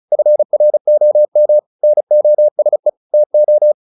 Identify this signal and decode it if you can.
Morse code, heard as a keyed tone, about 34 wpm, 600 Hz. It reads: FROM NOSE TO
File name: the old man and the sea 2 35wpm